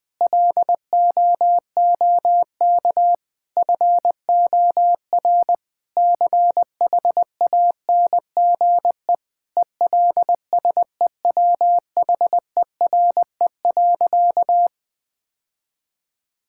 Morse code, heard as a keyed tone, about 21 wpm, 700 Hz. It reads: LOOK FOR CHANGE ELSEWHERE.
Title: Sentences from Top 1600 Words 20wpm